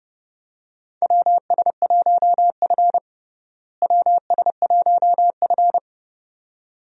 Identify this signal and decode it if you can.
Morse code, heard as a keyed tone, about 30 wpm, 700 Hz. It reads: WS1F WS1F
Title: USA Call Signs - 4 Characters - Repeated 2x - 30wpm